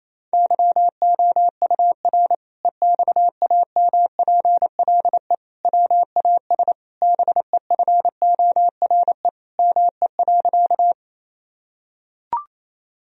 Morse code, heard as a keyed tone, about 28 wpm, 700 Hz. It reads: YOUR EXAMPLE WAS BEFORE ME.
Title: Sentences from Top 1000 Words - Review Part 1 - 28wpm